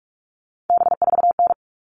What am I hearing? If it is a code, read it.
Morse code, heard as a keyed tone, about 44 wpm, 700 Hz. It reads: B4N
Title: All Cut Numbers - 3-Digit 45wpm